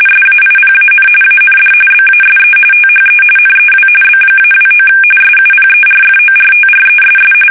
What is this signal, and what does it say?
RTTY, about 45.45 baud. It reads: SAIPAN AND BASE THE NEW B-29 SUPERFORTRESS
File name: fsk850 / battleshipnj